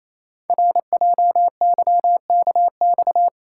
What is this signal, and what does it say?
Morse code, heard as a keyed tone, about 28 wpm, 700 Hz. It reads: RJYKX
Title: Code Groups - 5 Letters - 28wpm